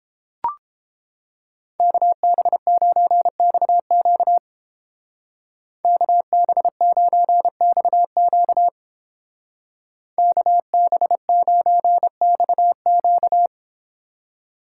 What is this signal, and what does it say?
Morse code, heard as a keyed tone, about 30 wpm, 700 Hz. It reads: KB9XQ KB9XQ KB9XQ